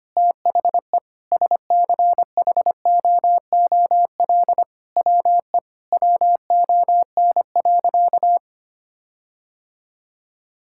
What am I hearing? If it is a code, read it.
Morse code, heard as a keyed tone, about 25 wpm, 700 Hz. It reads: THE SCHOOL WE WON.
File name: practice / 25wpm / Sentences with 9-Word Phrases - 25wpm